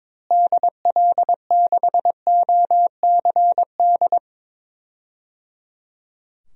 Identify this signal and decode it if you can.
Morse code, heard as a keyed tone, about 22 wpm, 700 Hz. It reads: DL6OCD
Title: DXCC-100 Call Signs - 6 Characters - Repeated 3x - 22wpm